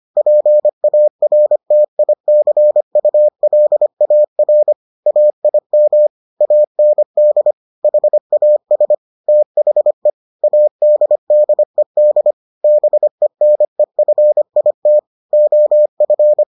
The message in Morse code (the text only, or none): PARTICULAR AIM AND HAS THE ADDED BENEFIT OF